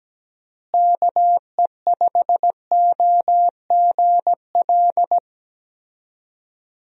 Morse code, heard as a keyed tone, about 17 wpm, 700 Hz. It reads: KE5OGL
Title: USA Call Signs - 6 Characters - Repeated 2x - 17wpm